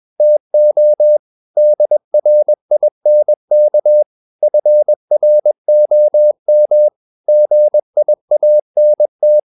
Morse code, heard as a keyed tone, about 21 wpm, 600 Hz. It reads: TO DRINK FROM GIANT
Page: CW med Gnister